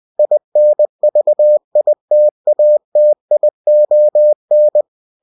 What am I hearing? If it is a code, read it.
Morse code, heard as a keyed tone, about 20 wpm, 600 Hz. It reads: INVITATION
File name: cw-books / winnie-the-pooh / winnie-20wpm-page-0105